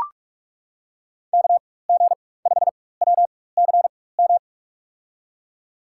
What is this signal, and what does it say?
Morse code, heard as a keyed tone, about 44 wpm, spaced as for 21 wpm, 700 Hz. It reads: KG5WCM
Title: USA Call Signs - 6 Characters - 4x Character Spacing - 45wpm